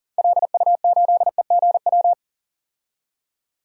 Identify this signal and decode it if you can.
Morse code, heard as a keyed tone, about 40 wpm, 700 Hz. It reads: LU8EGW
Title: DXCC-100 Call Signs - 6 Characters - 40wpm